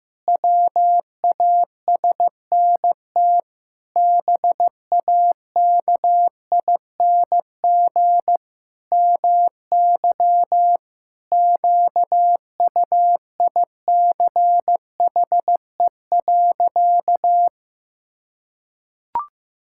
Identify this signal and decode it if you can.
Morse code, heard as a keyed tone, about 15 wpm, 700 Hz. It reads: WASNT BAKING MY QUICHE.
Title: Pangram Challenge v2 - 15wpm